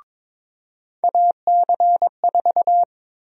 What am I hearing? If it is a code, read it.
Morse code, heard as a keyed tone, about 22 wpm, 700 Hz. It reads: AC4
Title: USA Call Sign Prefixes - 22wpm